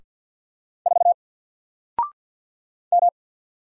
Morse code, heard as a keyed tone, about 48 wpm, 700 Hz. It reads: 4 M
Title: Single Character (Letter:Number:Punctuation:Prosign) 50wpm